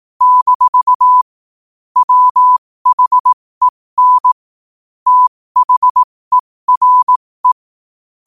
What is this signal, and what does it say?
Morse code, heard as a keyed tone, about 18 wpm, spaced as for 15 wpm, 1000 Hz. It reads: - WHEN THERE
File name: qotdfiles / 15wpm